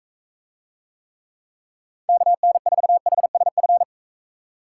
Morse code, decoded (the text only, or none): KN4HSF